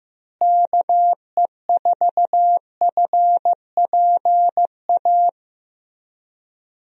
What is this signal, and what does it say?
Morse code, heard as a keyed tone, about 15 wpm, 700 Hz. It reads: KE4FPA
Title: USA Call Signs - 6 Characters - Repeated 3x - 15wpm